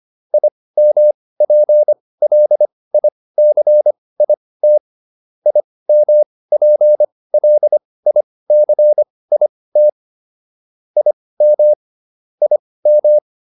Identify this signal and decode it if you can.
Morse code, heard as a keyed tone, about 25 wpm, spaced as for 18 wpm, 600 Hz. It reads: IMPLICIT IMPLICIT IM IM